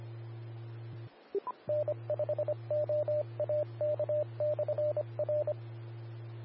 Morse code, heard as a keyed tone, about 25 wpm, 600 Hz, with radio noise. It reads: N5OAK/R
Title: Austin, TX Amateur Repeaters